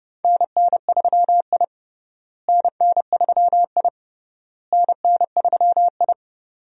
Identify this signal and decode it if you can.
Morse code, heard as a keyed tone, about 31 wpm, 700 Hz. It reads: NN3I NN3I NN3I
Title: USA Call Signs - 4 Characters - Repeated 3x - 30wpm